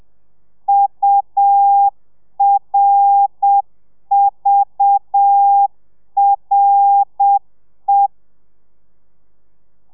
Morse code, heard as a keyed tone, about 7 wpm, 800 Hz. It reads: URVRE